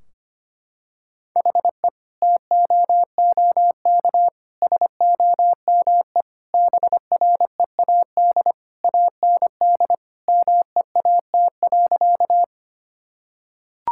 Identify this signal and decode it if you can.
Morse code, heard as a keyed tone, about 25 wpm, 700 Hz. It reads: HE TOOK SOME BREAD AND MEAT.
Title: Sentences from Top 1600 Words 25wpm